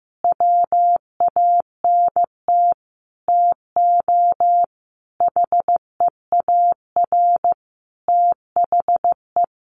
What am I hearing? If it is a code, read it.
Morse code, heard as a keyed tone, about 15 wpm, 700 Hz. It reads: WANT TO HEAR THE
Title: Sentences from Top 1400 Words 15wpm